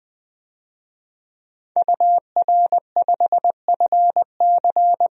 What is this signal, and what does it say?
Morse code, heard as a keyed tone, about 20 wpm, 700 Hz. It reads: UR5FC
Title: DXCC-100 Call Signs - 5 Characters - 20wpm